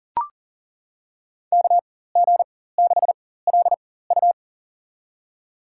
Morse code, heard as a keyed tone, about 40 wpm, spaced as for 19 wpm, 700 Hz. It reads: KG6LU
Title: USA Call Signs - 4x Character Spacing - 40wpm